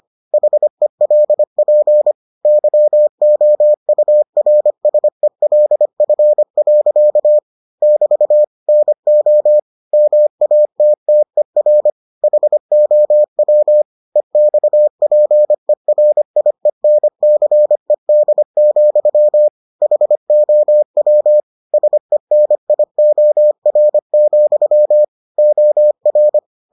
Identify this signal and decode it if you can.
Morse code, heard as a keyed tone, about 25 wpm, 600 Hz. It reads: HELP YOURSELF. <BT> NO MATTER HOW EXPERIENCED, HOW SENIOR, OR